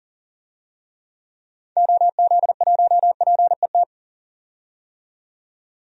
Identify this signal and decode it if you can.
Morse code, heard as a keyed tone, about 40 wpm, 700 Hz. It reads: OZ1PET
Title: DXCC-100 Call Signs - 6 Characters - 40wpm